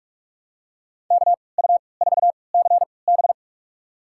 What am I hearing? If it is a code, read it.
Morse code, heard as a keyed tone, about 41 wpm, spaced as for 25 wpm, 700 Hz. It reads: KU4CB
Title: USA Call Signs - 5 Characters - 3x Character Spacing - 45wpm